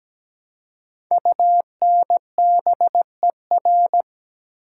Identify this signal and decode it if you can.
Morse code, heard as a keyed tone, about 17 wpm, 700 Hz. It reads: UNBER
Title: Code Groups - 5 Letters - 17wpm